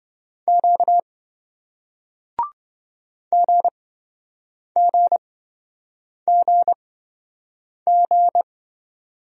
Morse code, heard as a keyed tone, about 27 wpm, spaced as for 7 wpm, 700 Hz. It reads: Q GGGG